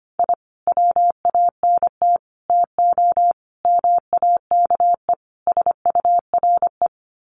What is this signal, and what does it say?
Morse code, heard as a keyed tone, about 25 wpm, 700 Hz. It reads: I WANT TO MAKE SURE
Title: Sets of 5 Words Encore2 25wpm